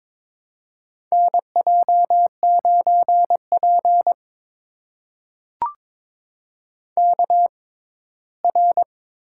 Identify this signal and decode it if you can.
Morse code, heard as a keyed tone, about 23 wpm, spaced as for 15 wpm, 700 Hz. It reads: NJ9P KR
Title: USA Call Signs - 4 Characters - 6x Character Spacing - 22wpm